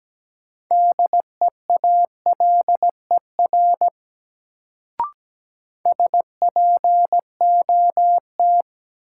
Morse code, heard as a keyed tone, about 17 wpm, 700 Hz. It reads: DEALER SPOT
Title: Sentences from Top 1100 Words - New Word Drill Alt - 17wpm